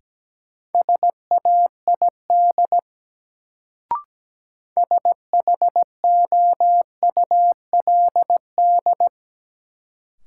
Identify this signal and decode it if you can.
Morse code, heard as a keyed tone, about 17 wpm, 700 Hz. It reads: SAID SHOULD
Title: Sets of 2 Words - New Word Drill - 17wpm